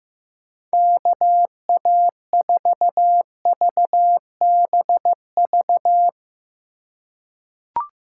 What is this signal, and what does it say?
Morse code, heard as a keyed tone, about 15 wpm, 700 Hz. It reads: KA4VBV
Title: USA Call Signs - 8x Character Spacing - 15wpm